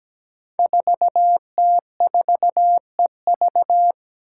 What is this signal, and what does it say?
Morse code, heard as a keyed tone, about 17 wpm, 700 Hz. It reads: 4T4EV
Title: Five Challenging Letter-Number Combinations 17wpm